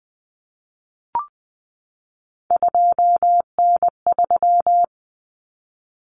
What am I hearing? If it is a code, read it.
Morse code, heard as a keyed tone, about 20 wpm, 700 Hz. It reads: 2N3
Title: Common Cut Numbers - 3-Digit 20wpm